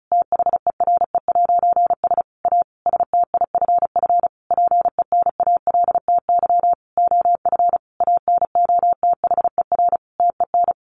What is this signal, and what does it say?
Morse code, heard as a keyed tone, about 34 wpm, 700 Hz. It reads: THERE'S A STIFF PENALTY OF ANOTHER TEN
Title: A Sound Of Thunder 35wpm